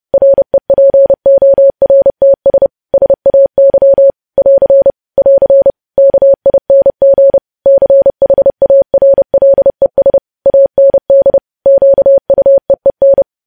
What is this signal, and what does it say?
Morse code, heard as a keyed tone, about 30 wpm, 550 Hz. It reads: REPORTS SAY <AR> <AR> KING CHARLES AND QUEEN
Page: News Headlines in Morse Code at 30 WPM